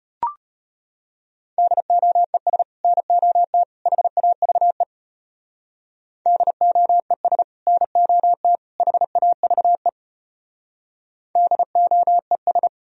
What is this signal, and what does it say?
Morse code, heard as a keyed tone, about 30 wpm, 700 Hz. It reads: DOES NOT HAVE DOES NOT HAVE DOES
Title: Speed-Racing - Sets of 3 Words 25wpm